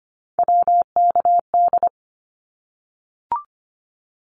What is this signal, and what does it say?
Morse code, heard as a keyed tone, about 25 wpm, 700 Hz. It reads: WKD
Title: Abbreviations Repeated 3x - 25wpm